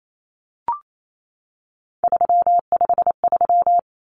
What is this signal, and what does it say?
Morse code, heard as a keyed tone, about 28 wpm, 700 Hz. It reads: 353
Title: Numbers - 3-Digit 28wpm